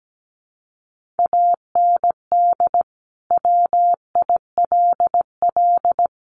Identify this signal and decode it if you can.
Morse code, heard as a keyed tone, about 17 wpm, 700 Hz. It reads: AND WILL
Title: Sentences from Top 1400 Words 17wpm